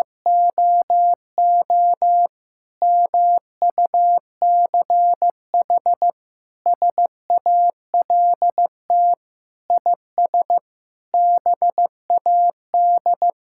Morse code, heard as a keyed tone, about 15 wpm, 700 Hz. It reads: OO MUCH SALT IS BAD
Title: Sentences from Top 1000 Words - Review Part 5 - 15wpm